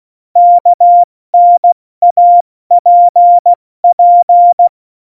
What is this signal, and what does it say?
Morse code, heard as a keyed tone, about 16 wpm, 700 Hz. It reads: KNAPP
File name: Morsekurs / Witze / Witz 26. Januar 2021, 70 F80